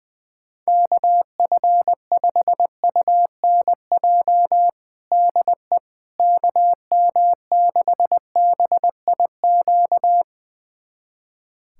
Morse code, heard as a keyed tone, about 20 wpm, 700 Hz. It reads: KF5UNJ DE KM6BIQ
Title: USA Callsign DE Callsign 20wpm